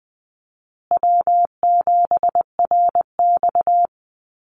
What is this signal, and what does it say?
Morse code, heard as a keyed tone, about 20 wpm, 700 Hz. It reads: W7RX
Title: Call Signs Repeated 2x - 20wpm